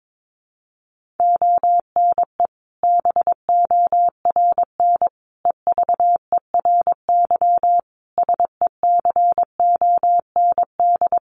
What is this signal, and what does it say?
Morse code, heard as a keyed tone, about 22 wpm, 700 Hz. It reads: ONE BORN EVERY SECOND
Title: Sentences from Top 1400 Words 22wpm